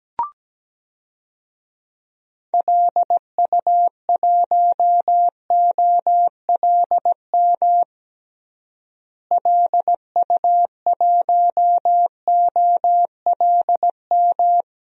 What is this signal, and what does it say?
Morse code, heard as a keyed tone, about 17 wpm, 700 Hz. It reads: LU1OLM LU1OLM
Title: DXCC-100 Call Signs - 6 Characters - Repeated 3x - 17wpm